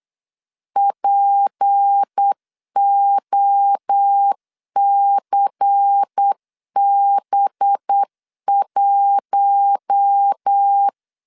Morse code, heard as a keyed tone, about 8 wpm, 800 Hz. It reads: POCB1
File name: LEZIONI / 6 PKX0 / 6 P 8wpm 3025